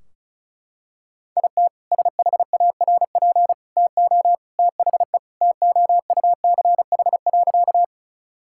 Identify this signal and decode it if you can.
Morse code, heard as a keyed tone, about 36 wpm, 700 Hz. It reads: IT SHARP TO THE TOUCH.
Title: Sentences from Top 1600 Words 35wpm